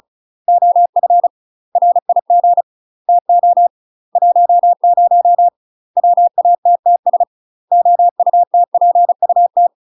Morse code, heard as a keyed tone, about 36 wpm, 700 Hz. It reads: OF RIG TO 10 WATTS OUTPUT
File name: UK MASTER QSOs TRAINING / 35WPM / UK MASTER QSOs 0022